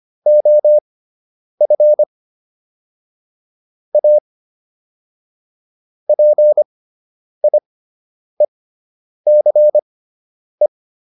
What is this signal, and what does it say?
Morse code, heard as a keyed tone, about 25 wpm, spaced as for 9 wpm, 600 Hz. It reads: OF A PIECE